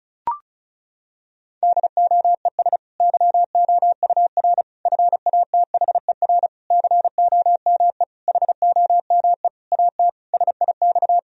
Morse code, heard as a keyed tone, about 34 wpm, 700 Hz. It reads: DOES YOUR FATHER COME HOME AT SIX
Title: Sentences from Top 1000 Words - Review Part 2 - 35wpm